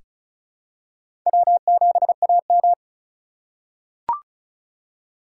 Morse code, heard as a keyed tone, about 36 wpm, 700 Hz. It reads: W7AM
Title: USA Call Signs - 4 Characters - 3x Character Spacing - 35wpm